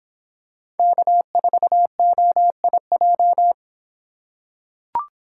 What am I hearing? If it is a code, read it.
Morse code, heard as a keyed tone, about 24 wpm, 700 Hz. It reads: K4OIJ